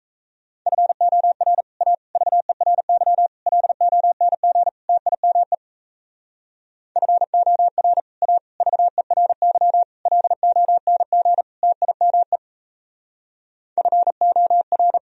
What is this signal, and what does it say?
Morse code, heard as a keyed tone, about 33 wpm, 700 Hz. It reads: FOR A VERY LONG TIME FOR A VERY LONG TIME FOR
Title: Speed-Racing - Sets of 5 Words 28wpm